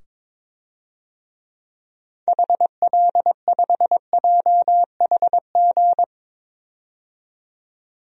Morse code, heard as a keyed tone, about 21 wpm, 700 Hz. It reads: HL5JHG
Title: DXCC-100 Call Signs - 6 Characters - 22wpm